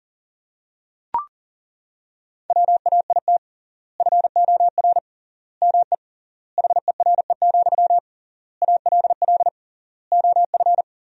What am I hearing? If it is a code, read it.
Morse code, heard as a keyed tone, about 37 wpm, 700 Hz. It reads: WAIT FOR ME HERE, ALL OF
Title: 3x Word-Spacing - Sentences from Top 500 Words - 40wpm